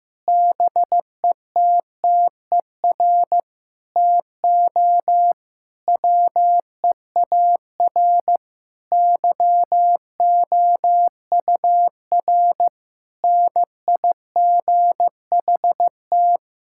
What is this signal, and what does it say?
Morse code, heard as a keyed tone, about 15 wpm, 700 Hz. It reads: BETTER TO WEAR YOUR NIGHT
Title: Sentences from Top 1600 Words 15wpm